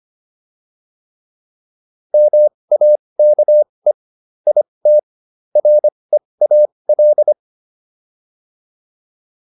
Morse code, heard as a keyed tone, about 25 wpm, spaced as for 20 wpm, 600 Hz. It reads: MAKE IT REAL